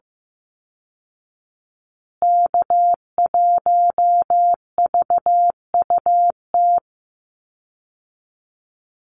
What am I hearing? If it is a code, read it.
Morse code, heard as a keyed tone, about 14 wpm, 700 Hz. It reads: K1VUT
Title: Call Signs Repeated 2x - 15wpm